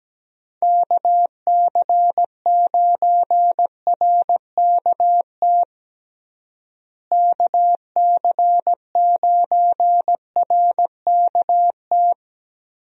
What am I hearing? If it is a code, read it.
Morse code, heard as a keyed tone, about 17 wpm, 700 Hz. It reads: KC9RKT KC9RKT
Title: USA Call Signs - 6 Characters - Repeated 2x - 17wpm